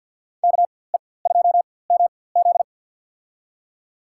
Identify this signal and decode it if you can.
Morse code, heard as a keyed tone, about 46 wpm, spaced as for 23 wpm, 700 Hz. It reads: KE2MZ
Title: USA Call Signs - 4x Character Spacing - 50wpm